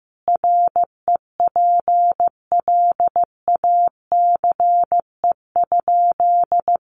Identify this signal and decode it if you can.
Morse code, heard as a keyed tone, about 15 wpm, 700 Hz. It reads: REPLACE?
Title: Sentences from Top 1400 Words 15wpm